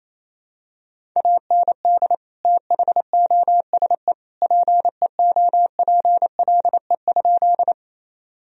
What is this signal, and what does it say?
Morse code, heard as a keyed tone, about 28 wpm, 700 Hz. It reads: AND THOSE PEOPLE?
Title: Sentences from Top 1000 Words - Review Part 1 - 28wpm